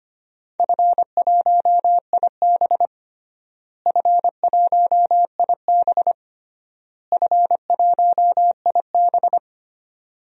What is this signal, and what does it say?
Morse code, heard as a keyed tone, about 25 wpm, 700 Hz. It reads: F1IB F1IB F1IB
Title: DXCC-100 Call Signs - 4 Characters - Repeated 3x - 25wpm